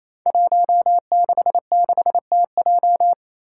Morse code, heard as a keyed tone, about 28 wpm, 700 Hz. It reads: 166TJ
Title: Five Challenging Letter-Number Combinations 28wpm